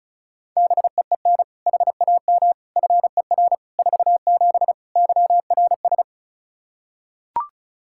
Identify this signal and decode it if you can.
Morse code, heard as a keyed tone, about 34 wpm, 700 Hz. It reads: BEEN HAM FER 47 YRS
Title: Practice QSO - Exchange 3 - More Info - 35wpm